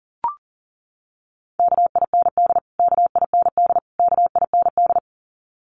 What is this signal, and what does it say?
Morse code, heard as a keyed tone, about 40 wpm, 700 Hz. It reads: KIND KIND KIND
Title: Top 200 Words Repeated 40wpm